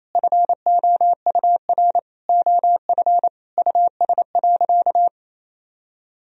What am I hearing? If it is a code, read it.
Morse code, heard as a keyed tone, about 28 wpm, 700 Hz. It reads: FOUR OF US.
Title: Sentences from Top 1000 Words - Review Part 1 - 28wpm